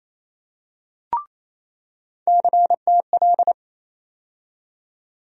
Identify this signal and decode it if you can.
Morse code, heard as a keyed tone, about 28 wpm, 700 Hz. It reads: CTL
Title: USA Call Sign Postfixes - 3 Letters 28wpm